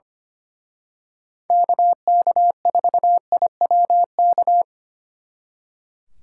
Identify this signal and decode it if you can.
Morse code, heard as a keyed tone, about 29 wpm, 700 Hz. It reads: KK4IWK
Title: Speed Racing - USA 2x3 Call Signs 25wpm